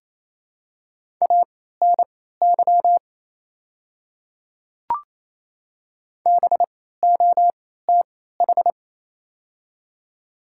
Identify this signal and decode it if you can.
Morse code, heard as a keyed tone, about 28 wpm, spaced as for 16 wpm, 700 Hz. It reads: ANY BOTH
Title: 3x Character Spacing - Top 300 Words - 28wpm